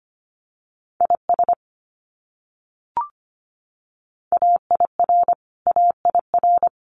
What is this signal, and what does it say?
Morse code, heard as a keyed tone, about 25 wpm, 700 Hz. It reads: IS AIR AIR
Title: Top 300 Words Repeated 25wpm